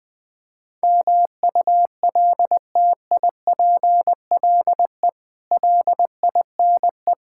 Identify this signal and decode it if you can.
Morse code, heard as a keyed tone, about 20 wpm, 700 Hz. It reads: MULTIPLE LINE
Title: Build A Phrase - 5-Word Phrases - 20wpm